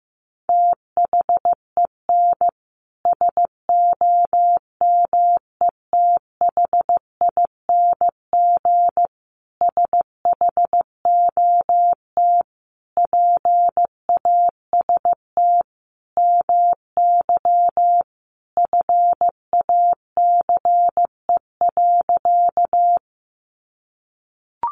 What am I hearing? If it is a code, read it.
Morse code, heard as a keyed tone, about 14 wpm, 700 Hz. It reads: THEN SOMETHING SHOT PAST MY FACE.
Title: Sentences from Top 1100 Words 15wpm